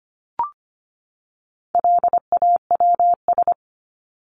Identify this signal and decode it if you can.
Morse code, heard as a keyed tone, about 25 wpm, 700 Hz. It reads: LAWS
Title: ES:S Suffix - Top 100 Words with Base Words 25wpm